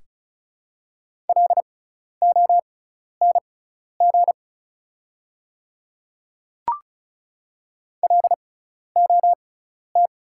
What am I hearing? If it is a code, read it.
Morse code, heard as a keyed tone, about 36 wpm, spaced as for 12 wpm, 700 Hz. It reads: LONG LOT
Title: 6x Character Spacing - Top 500 Words - 35wpm